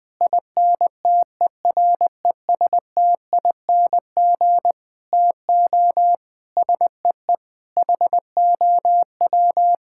Morse code, heard as a keyed tone, about 20 wpm, 700 Hz. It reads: INTERESTING TO SEE HOW
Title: Sets of 8 Words - 20wpm